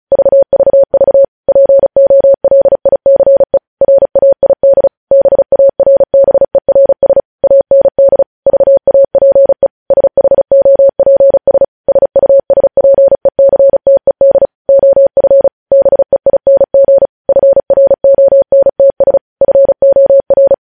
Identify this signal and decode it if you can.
Morse code, heard as a keyed tone, about 36 wpm, 550 Hz. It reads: VVV POLICE RAID BARBERS AND VAPE SHOPS SUSPECTED OF BEING FRONTS FOR
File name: podcast / News headlines-1x-35x35